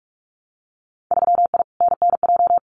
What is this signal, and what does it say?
Morse code, heard as a keyed tone, about 46 wpm, 700 Hz. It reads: 3I NNJ
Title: Winter Field Day Exchanges 45wpm